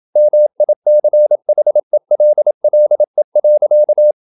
Morse code, heard as a keyed tone, about 27 wpm, 600 Hz. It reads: MICHELLE.
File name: Ferie QRQ 27wpm